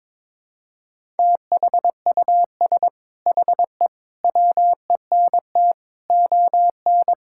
Morse code, heard as a keyed tone, about 22 wpm, 700 Hz. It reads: THUS HE WENT ON
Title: Sentences from Top 1000 Words - Review Part 5 - 22wpm